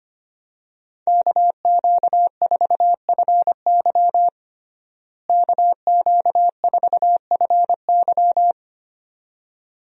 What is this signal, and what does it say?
Morse code, heard as a keyed tone, about 25 wpm, 700 Hz. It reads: KQ4FY KQ4FY
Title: USA Call Signs - 5 Characters - Repeated 2x - 25wpm